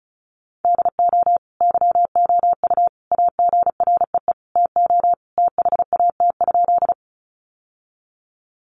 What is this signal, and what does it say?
Morse code, heard as a keyed tone, about 34 wpm, 700 Hz. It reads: DO YOU AGREE TO THAT?
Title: Sentences from Top 1100 Words 35wpm